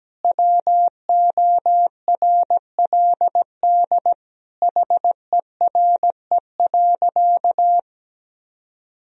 Morse code, heard as a keyed tone, about 17 wpm, 700 Hz. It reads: WORLD HERE.
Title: Sentences from Top 1000 Words - Review Part 3 - 17wpm